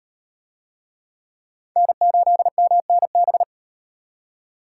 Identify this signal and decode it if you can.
Morse code, heard as a keyed tone, about 35 wpm, 700 Hz. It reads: N8MNB